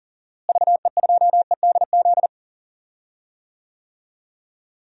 Morse code, heard as a keyed tone, about 40 wpm, 700 Hz. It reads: VE2EDZ
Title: DXCC-100 Call Signs - 6 Characters - Repeated 3x - 40wpm